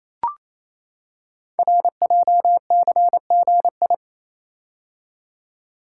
Morse code, heard as a keyed tone, about 29 wpm, 700 Hz. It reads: RJCGI
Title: Code Groups - 5 Letters - 28wpm